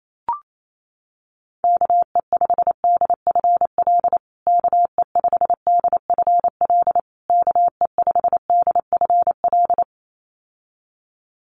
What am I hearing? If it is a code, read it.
Morse code, heard as a keyed tone, about 28 wpm, 700 Hz. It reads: KE5DFL KE5DFL KE5DFL
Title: USA Call Signs Repeated 3x 28wpm